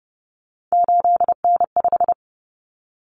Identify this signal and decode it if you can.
Morse code, heard as a keyed tone, about 30 wpm, 700 Hz. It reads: 8N5
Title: Common Cut Numbers - 3-Digit 30wpm